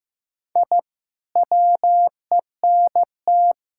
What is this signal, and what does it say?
Morse code, heard as a keyed tone, about 14 wpm, 700 Hz. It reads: I WENT
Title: Sentences from Top 1000 Words - Review Part 6 - 15wpm